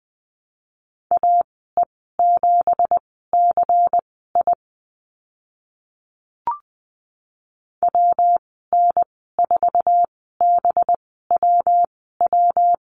Morse code, heard as a keyed tone, about 20 wpm, spaced as for 14 wpm, 700 Hz. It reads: AE7CI WN4BWW
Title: 213 Lesson 34g - Number 8 - Call Signs 20wpm